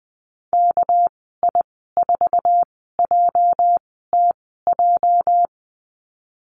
Morse code, heard as a keyed tone, about 20 wpm, spaced as for 14 wpm, 700 Hz. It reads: KI4JTJ
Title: 238 Lesson 38g - Letter q - Call Signs 20wpm